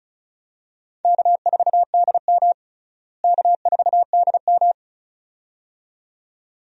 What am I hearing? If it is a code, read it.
Morse code, heard as a keyed tone, about 36 wpm, 700 Hz. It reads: K4DM K4DM
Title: USA Call Signs - 4 Characters - Repeated 2x - 35wpm